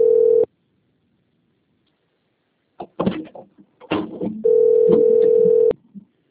Ringback tone (North America), 440 + 480 Hz.